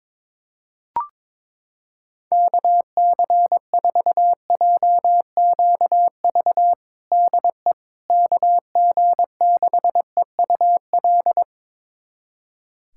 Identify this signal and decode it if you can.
Morse code, heard as a keyed tone, about 22 wpm, 700 Hz. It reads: KC4JQV DE KG6EUL
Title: USA Callsign DE Callsign - 6 Characters - 22wpm